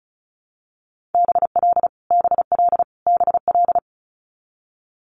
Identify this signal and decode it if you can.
Morse code, heard as a keyed tone, about 36 wpm, 700 Hz. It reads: BL BL BL
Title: Multi-letter Phonemes Repeated 3x 35wpm